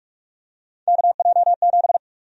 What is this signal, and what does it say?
Morse code, heard as a keyed tone, about 46 wpm, 700 Hz. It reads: KJ7
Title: USA Call Sign Prefixes - Repeated 2x - 45wpm